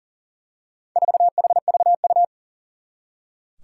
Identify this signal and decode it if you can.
Morse code, heard as a keyed tone, about 38 wpm, 700 Hz. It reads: 4HVU
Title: Four Challenging Letter-Number Combinations 40wpm